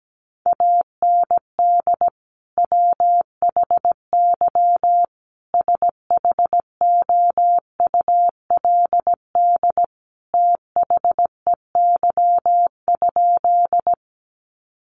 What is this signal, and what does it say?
Morse code, heard as a keyed tone, about 17 wpm, 700 Hz. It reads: AND WHY SHOULD THEY?
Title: Sentences from Top 300 Words 17wpm